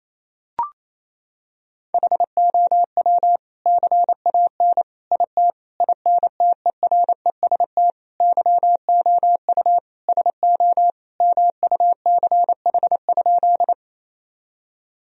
Morse code, heard as a keyed tone, about 28 wpm, 700 Hz. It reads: HOW CAN IT INTEREST YOU SO MUCH?
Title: Sentences from Top 1000 Words - Review Part 2 - 28wpm